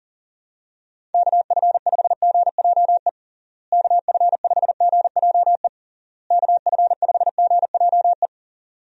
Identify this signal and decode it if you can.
Morse code, heard as a keyed tone, about 40 wpm, 700 Hz. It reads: KF5GJE KF5GJE KF5GJE
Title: USA Call Signs - 6 Characters - Repeated 3x - 40wpm